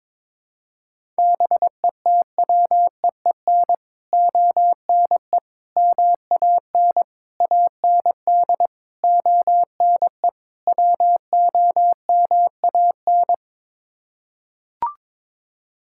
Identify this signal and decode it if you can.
Morse code, heard as a keyed tone, about 22 wpm, 700 Hz. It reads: BETWEEN ONE MAN AND ONE WOMAN
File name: practice / 22wpm / Sets of 6 Words - 22wpm